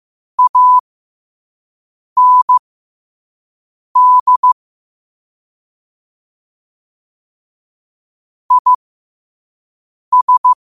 Morse code, heard as a keyed tone, about 15 wpm, spaced as for 5 wpm, 1000 Hz. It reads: AND IS
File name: qotdfiles / 5wpm